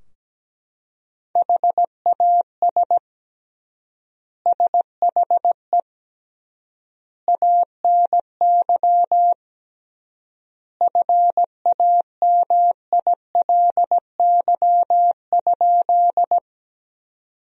Morse code, heard as a keyed tone, about 17 wpm, 700 Hz. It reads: HAS SHE ANY FAMILY?
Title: 3x Word-Spacing - Sentences from Top 400 Words - 17wpm